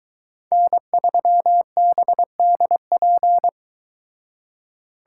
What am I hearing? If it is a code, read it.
Morse code, heard as a keyed tone, about 22 wpm, 700 Hz. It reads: N3BDP